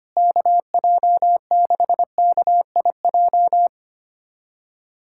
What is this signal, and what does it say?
Morse code, heard as a keyed tone, about 25 wpm, 700 Hz. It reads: KJ6KIJ
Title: USA Callsign DE Callsign - 6 Characters - 25wpm